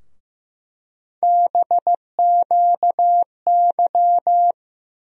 Morse code, heard as a keyed tone, about 15 wpm, 700 Hz. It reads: BQY
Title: USA Call Sign Postfixes - 3 Letters 15wpm